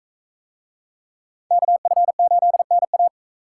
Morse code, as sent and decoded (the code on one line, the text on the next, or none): -.- ..-. ---.. -. .-
KF8NA